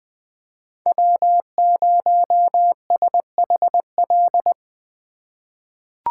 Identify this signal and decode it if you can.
Morse code, heard as a keyed tone, about 20 wpm, 700 Hz. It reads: W0SHL
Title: K1USN SST QSO Practice - 20wpm